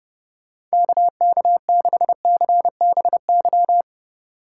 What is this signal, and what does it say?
Morse code, heard as a keyed tone, about 31 wpm, 700 Hz. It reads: KK6CBY